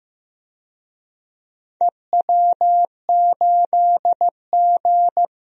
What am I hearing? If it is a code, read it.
Morse code, heard as a keyed tone, about 15 wpm, 700 Hz. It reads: EW8G
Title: DXCC-100 Call Signs - 4 Characters - Repeated 3x - 15wpm